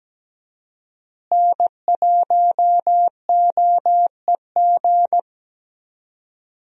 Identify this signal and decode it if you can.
Morse code, heard as a keyed tone, about 17 wpm, 700 Hz. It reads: N1OEG
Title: USA Call Signs - 5 Characters - Repeated 2x - 17wpm